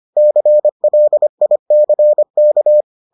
Morse code, heard as a keyed tone, about 25 wpm, 600 Hz. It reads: CLICK